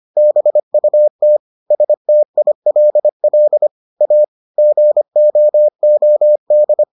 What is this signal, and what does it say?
Morse code, heard as a keyed tone, about 25 wpm, 600 Hz. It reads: BUT STILL A GOOD